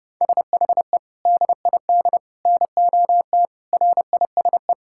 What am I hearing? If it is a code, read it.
Morse code, heard as a keyed tone, about 30 wpm, 700 Hz. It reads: SHE DID NOT RISE
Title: Sentences from Top 1000 Words - Review Part 6 - 30wpm